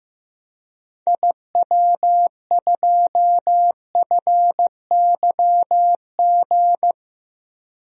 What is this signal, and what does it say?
Morse code, heard as a keyed tone, about 16 wpm, 700 Hz. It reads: IW2FYG
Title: DXCC-100 Call Signs - 6 Characters - Repeated 3x - 15wpm